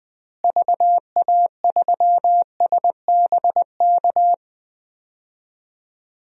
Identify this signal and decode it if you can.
Morse code, heard as a keyed tone, about 20 wpm, 700 Hz. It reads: VA3SBK
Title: DXCC-100 Call Signs - 6 Characters - Repeated 2x - 20wpm